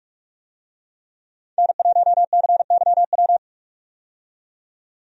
Morse code, heard as a keyed tone, about 42 wpm, 700 Hz. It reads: N1CYW
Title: USA Call Signs - 5 Characters - 45wpm